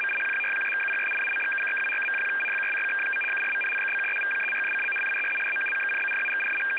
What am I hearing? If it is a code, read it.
Two-tone teleprinter signal, about 850 Hz shift.
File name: US-INTEL 100Bd-800Hz FSK